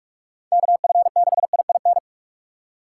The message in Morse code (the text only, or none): KF6IIN